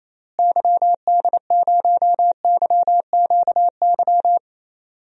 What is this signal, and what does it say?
Morse code, heard as a keyed tone, about 28 wpm, 700 Hz. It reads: YD0YQY
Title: DXCC-100 Call Signs - 6 Characters - Repeated 2x - 28wpm